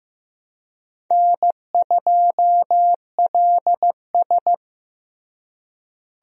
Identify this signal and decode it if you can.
Morse code, heard as a keyed tone, about 15 wpm, 700 Hz. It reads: N2LS
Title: USA Call Signs - 4 Characters - Repeated 2x - 15wpm